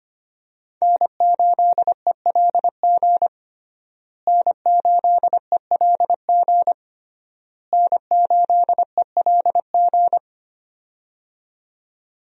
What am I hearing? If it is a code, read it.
Morse code, heard as a keyed tone, about 25 wpm, 700 Hz. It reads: N8ELG N8ELG N8ELG
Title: USA Call Signs - 5 Characters - Repeated 3x - 25wpm